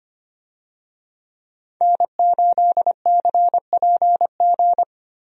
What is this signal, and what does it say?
Morse code, heard as a keyed tone, about 25 wpm, 700 Hz. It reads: N8CPG
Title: USA Call Signs - 5 Characters - Repeated 3x - 25wpm